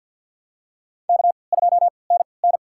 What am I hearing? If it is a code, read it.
Morse code, heard as a keyed tone, about 53 wpm, spaced as for 29 wpm, 700 Hz. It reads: K2NN
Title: USA Call Signs - 4 Characters - 3x Character Spacing - 50wpm